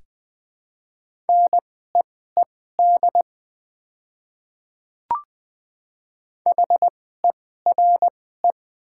Morse code, heard as a keyed tone, about 20 wpm, spaced as for 14 wpm, 700 Hz. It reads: NEED HERE
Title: 2x Character Spacing - Top 500 Words - 20wpm